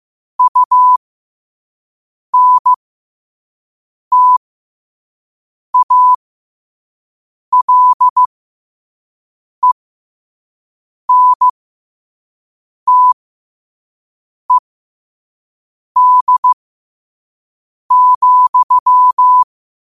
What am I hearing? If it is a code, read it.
Morse code, heard as a keyed tone, about 15 wpm, spaced as for 5 wpm, 1000 Hz. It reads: UNTALENTED,
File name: qotdfiles / 5wpm